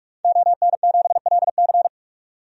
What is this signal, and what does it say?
Morse code, heard as a keyed tone, about 41 wpm, 700 Hz. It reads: ON7LC
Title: DXCC-100 Call Signs - 5 Characters - 45wpm